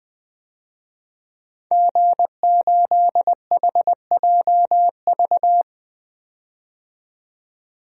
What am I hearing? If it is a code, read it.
Morse code, heard as a keyed tone, about 20 wpm, 700 Hz. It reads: G8HJV
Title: DXCC-100 Call Signs - 5 Characters - 20wpm